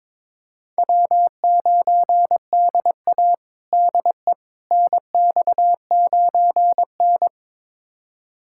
Morse code, heard as a keyed tone, about 22 wpm, 700 Hz. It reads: W9DA DE NX9N
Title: USA Callsign DE Callsign - 4 Characters - 22wpm